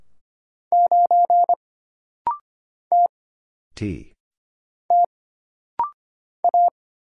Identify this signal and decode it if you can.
Morse code, heard as a keyed tone, about 25 wpm, 700 Hz. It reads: 9 T T A
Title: Single Letter-Number Alt - Rapid-Fire 25wpm